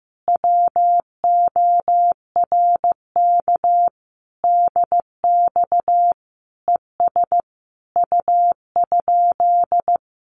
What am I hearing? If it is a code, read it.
Morse code, heard as a keyed tone, about 14 wpm, 700 Hz. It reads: WORK DX ES U?
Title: Common QSO Phrases Encore 15wpm